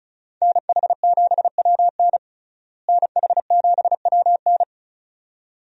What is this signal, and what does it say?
Morse code, heard as a keyed tone, about 36 wpm, 700 Hz. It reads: NH7WN NH7WN
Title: USA Call Signs - 5 Characters - Repeated 2x - 35wpm